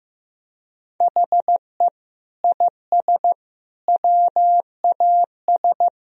Morse code, heard as a keyed tone, about 15 wpm, 700 Hz. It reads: HE IS WAS
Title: Sentences from Top 1000 Words - Review Part 2 - 15wpm